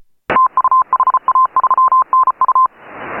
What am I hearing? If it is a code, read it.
Morse code, heard as a keyed tone, about 34 wpm, 1040 Hz. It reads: TU HA3NU
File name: AUDIO2018 / HA3FMR CW